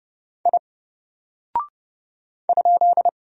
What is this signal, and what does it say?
Morse code, heard as a keyed tone, about 30 wpm, 700 Hz. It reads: I ?
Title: Single Character (Letter:Number:Punctuation:Prosign) 30wpm